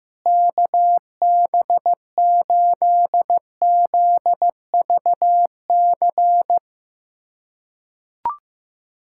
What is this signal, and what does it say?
Morse code, heard as a keyed tone, about 15 wpm, 700 Hz. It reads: KB8ZVC
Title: USA Callsign DE Callsign - 6 Characters - 15wpm